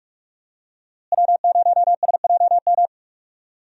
Morse code, heard as a keyed tone, about 46 wpm, 700 Hz. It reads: W0SJM
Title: USA Call Signs - 5 Characters - Repeated 3x - 45wpm